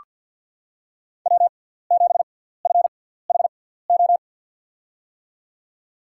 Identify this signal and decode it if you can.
Morse code, heard as a keyed tone, about 48 wpm, spaced as for 17 wpm, 700 Hz. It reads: W7FHO
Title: USA Call Signs - 5 Characters - 6x Character Spacing - 50wpm